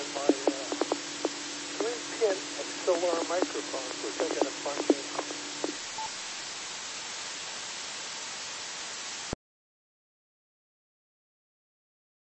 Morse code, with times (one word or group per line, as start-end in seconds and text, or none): none